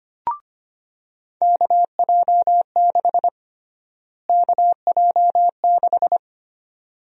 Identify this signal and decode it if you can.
Morse code, heard as a keyed tone, about 25 wpm, 700 Hz. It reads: KJ6 KJ6
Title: USA Call Sign Prefixes - Repeated 2x - 25wpm